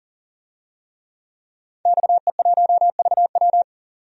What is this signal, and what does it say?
Morse code, heard as a keyed tone, about 40 wpm, 700 Hz. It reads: XE1VW
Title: DXCC-100 Call Signs - 5 Characters - 40wpm